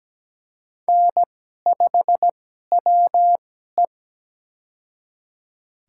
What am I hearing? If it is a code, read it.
Morse code, heard as a keyed tone, about 17 wpm, spaced as for 12 wpm, 700 Hz. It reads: N5WE
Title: USA Call Signs - 2x Character Spacing - 17wpm